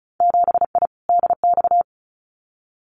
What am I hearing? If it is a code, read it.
Morse code, heard as a keyed tone, about 34 wpm, 700 Hz. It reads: 7I DX
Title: Winter Field Day Exchanges Encore 35wpm